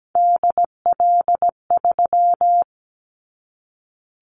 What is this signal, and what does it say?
Morse code, heard as a keyed tone, about 17 wpm, 700 Hz. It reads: DL3
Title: Callsign Prefixes for Top 50 DXCC - 17wpm